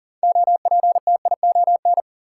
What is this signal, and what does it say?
Morse code, heard as a keyed tone, about 40 wpm, 700 Hz. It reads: OPTION
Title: Sentences from Top 1300 Words - New Word Drill Alt - 40wpm